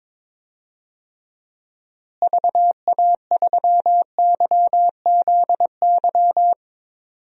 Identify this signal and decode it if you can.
Morse code, heard as a keyed tone, about 22 wpm, 700 Hz. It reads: VA3YZY
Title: DXCC-100 Call Signs - 6 Characters - Repeated 2x - 22wpm